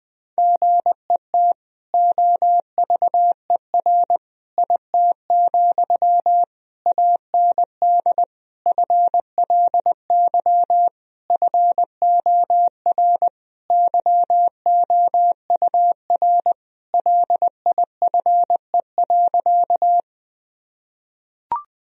Morse code, heard as a keyed tone, about 20 wpm, 700 Hz. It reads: GET OVER IT, AND FLY FOR YOUR LIFE.
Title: Sentences from Top 1000 Words - Review Part 2 - 20wpm